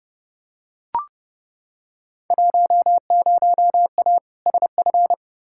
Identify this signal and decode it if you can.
Morse code, heard as a keyed tone, about 30 wpm, 700 Hz. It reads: 10A SF
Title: Field Day Exchanges Encore - 30wpm